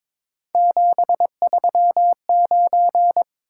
Morse code, heard as a keyed tone, about 22 wpm, 700 Hz. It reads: 739
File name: practice / 22wpm / Three Letter-Number Combinations - 22wpm